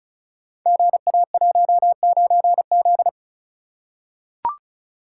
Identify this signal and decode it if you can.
Morse code, heard as a keyed tone, about 36 wpm, 700 Hz. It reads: GA19Z
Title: Code Groups - 5 Letter:Numbers - 35wpm